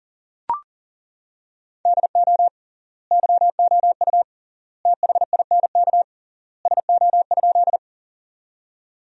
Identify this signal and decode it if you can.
Morse code, heard as a keyed tone, about 40 wpm, 700 Hz. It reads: DO YOU THINK SO?
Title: 3x Word-Spacing - Sentences from Top 200 Words - 40wpm